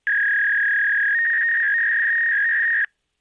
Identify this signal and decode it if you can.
Fax machine handshake.
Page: telephone-fax-machine